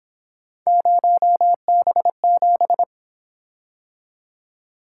Morse code, heard as a keyed tone, about 26 wpm, 700 Hz. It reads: 0B7
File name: Speed-Racing - Three Letter-Number Combinations - 22wpm